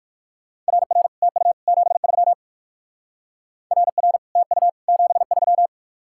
Morse code, heard as a keyed tone, about 59 wpm, 700 Hz. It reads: RR TU 73 RR TU 73